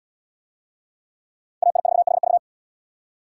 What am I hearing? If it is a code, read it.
Morse code, heard as a keyed tone, about 73 wpm, 700 Hz. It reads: AI4HV